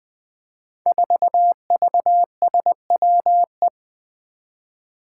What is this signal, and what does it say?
Morse code, heard as a keyed tone, about 20 wpm, 700 Hz. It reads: 4VSWE
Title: Five Challenging Letter-Number Combinations 20wpm